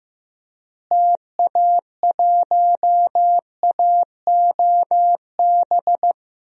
Morse code, heard as a keyed tone, about 15 wpm, 700 Hz. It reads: TA1AOB
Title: DXCC-100 Call Signs - 6 Characters - Repeated 2x - 15wpm